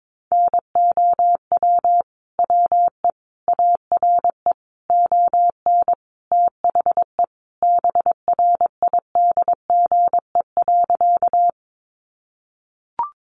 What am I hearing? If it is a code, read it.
Morse code, heard as a keyed tone, about 22 wpm, 700 Hz. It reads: NOW WE ARE ON THE BRIDGE.
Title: Sentences from Top 1200 Words 22wpm